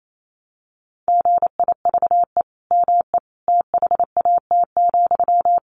Morse code, heard as a keyed tone, about 28 wpm, 700 Hz. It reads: GIVE ME THAT,
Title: Sentences from Top 1100 Words 28wpm